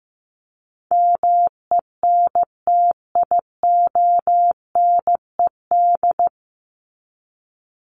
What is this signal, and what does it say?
Morse code, heard as a keyed tone, about 15 wpm, 700 Hz. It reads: MENTIONED
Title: ED Suffix - Top 100 Words with Base Words 15wpm